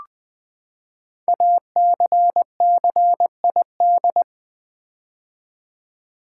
Morse code, heard as a keyed tone, about 20 wpm, 700 Hz. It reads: ACCID
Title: Build A Word - 5 Letters At A Time - 12-Letter Words - 20wpm